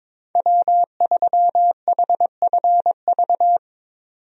Morse code, heard as a keyed tone, about 22 wpm, 700 Hz. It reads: W3HFV
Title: USA Call Signs - 5 Characters - 22wpm